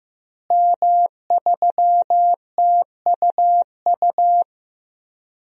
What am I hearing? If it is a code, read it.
Morse code, heard as a keyed tone, about 16 wpm, 700 Hz. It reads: M3TUU
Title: DXCC-100 Call Signs - 5 Characters - Repeated 2x - 15wpm